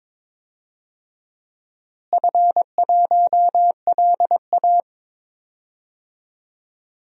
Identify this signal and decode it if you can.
Morse code, heard as a keyed tone, about 22 wpm, 700 Hz. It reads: F1LA
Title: DXCC-100 Call Signs - 4 Characters - Repeated 3x - 22wpm